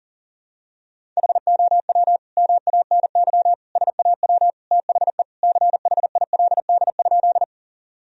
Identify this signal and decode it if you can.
Morse code, heard as a keyed tone, about 40 wpm, 700 Hz. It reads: HOW MANY SAW THE CHILD?
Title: Sentences from Top 1000 Words - Review Part 1 - 40wpm